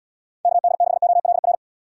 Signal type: Morse code

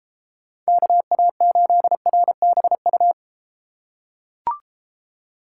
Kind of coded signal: Morse code

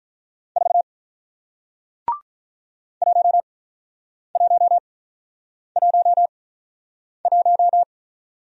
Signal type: Morse code

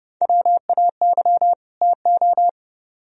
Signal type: Morse code

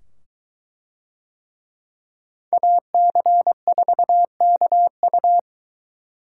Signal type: Morse code

